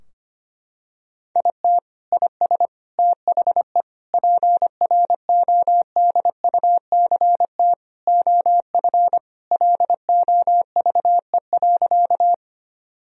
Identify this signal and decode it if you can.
Morse code, heard as a keyed tone, about 25 wpm, 700 Hz. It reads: IT IS THE PRODUCT OF LOVE.
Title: Sentences from Top 1000 Words - Review Part 2 - 25wpm